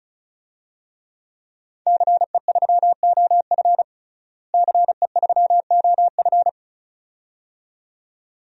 Morse code, heard as a keyed tone, about 36 wpm, 700 Hz. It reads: CE3OF CE3OF
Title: DXCC-100 Call Signs - 5 Characters - Repeated 2x - 35wpm